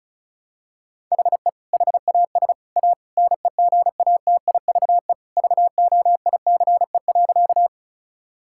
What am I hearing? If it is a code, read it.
Morse code, heard as a keyed tone, about 36 wpm, 700 Hz. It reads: HE HAS A NEGATIVE VOICE.
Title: Sentences from Top 1000 Words - Review Part 6 - 35wpm